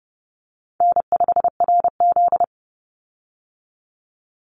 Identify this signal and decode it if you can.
Morse code, heard as a keyed tone, about 30 wpm, 700 Hz. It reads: N5RZ
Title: Easy Call Signs Repeated 3x - 30wpm